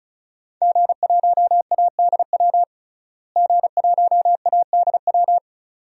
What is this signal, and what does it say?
Morse code, heard as a keyed tone, about 36 wpm, 700 Hz. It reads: G1ADW G1ADW
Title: DXCC-100 Call Signs - 5 Characters - Repeated 2x - 35wpm